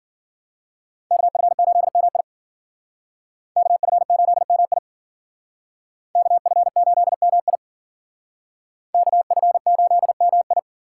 Morse code, heard as a keyed tone, about 51 wpm, 700 Hz. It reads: KF8MI KF8MI KF8MI KF8MI